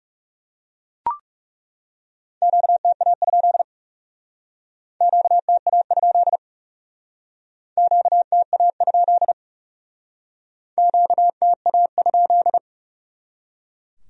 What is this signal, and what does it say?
Morse code, heard as a keyed tone, about 38 wpm, 700 Hz. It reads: QTA? QTA? QTA? QTA?